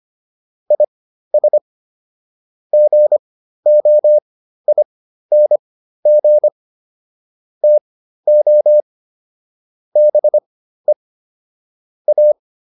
Morse code, heard as a keyed tone, about 25 wpm, spaced as for 13 wpm, 600 Hz. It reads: IS GOING TO BE A